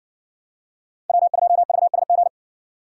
Morse code, heard as a keyed tone, about 57 wpm, 700 Hz. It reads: W2VSG